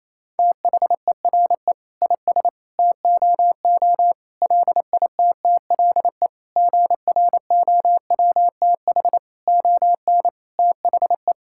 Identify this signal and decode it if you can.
Morse code, heard as a keyed tone, about 27 wpm, 700 Hz. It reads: THERE IS TOO LITTLE GROWTH ON THE
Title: Sentences from Top 1000 Words - Review Part 5 - 28wpm